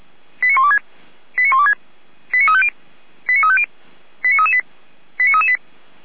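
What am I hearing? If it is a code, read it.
Five-tone selective call (ZVEI), heard as quick runs of five tones, tones of about 80 ms. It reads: 89126 89126 89350 89350 89338 89338
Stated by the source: Five-tone ZVEI selective call received on SDR, fire brigade Feuerwehr Orlamünde.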